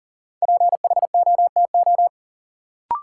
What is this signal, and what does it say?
Morse code, heard as a keyed tone, about 40 wpm, 700 Hz. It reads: PHOTO
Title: Sentences from Top 1200 Words - New Word Drill - 40wpm